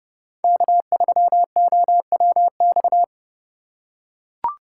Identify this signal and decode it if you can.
Morse code, heard as a keyed tone, about 30 wpm, 700 Hz. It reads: K3OWX